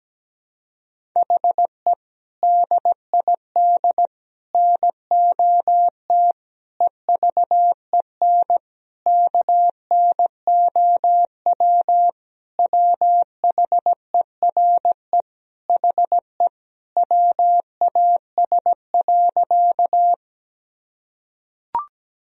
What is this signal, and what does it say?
Morse code, heard as a keyed tone, about 17 wpm, 700 Hz. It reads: HE DID NOT EVEN KNOW WHERE HE WAS.
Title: Sentences from Top 1000 Words - Review Part 1 - 17wpm